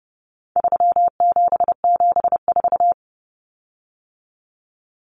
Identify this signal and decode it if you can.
Morse code, heard as a keyed tone, about 30 wpm, 700 Hz. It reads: 3774
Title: Common Cut Numbers - 4-Digit 30wpm